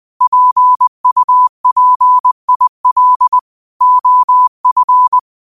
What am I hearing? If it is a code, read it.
Morse code, heard as a keyed tone, about 20 wpm, 1000 Hz. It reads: PUPIL OF
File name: qotdfiles / 20wpm